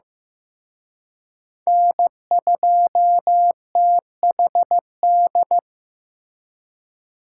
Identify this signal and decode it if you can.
Morse code, heard as a keyed tone, about 18 wpm, 700 Hz. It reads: N2THD
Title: Speed Racing - USA 1x3 Call Signs 15wpm